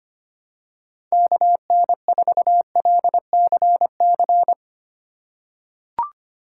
Morse code, heard as a keyed tone, about 25 wpm, 700 Hz. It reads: KN4LCC
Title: USA Call Signs - 6x Character Spacing - 25wpm